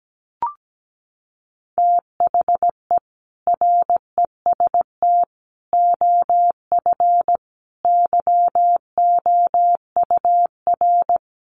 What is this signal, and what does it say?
Morse code, heard as a keyed tone, about 17 wpm, 700 Hz. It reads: THE REST OF YOUR
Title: Sets of 5 Words 17wpm